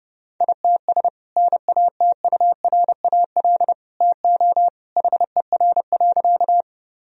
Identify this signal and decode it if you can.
Morse code, heard as a keyed tone, about 32 wpm, 700 Hz. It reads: ITS NATURAL TO HER.
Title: Sentences from Top 1000 Words - Review Part 4 - 30wpm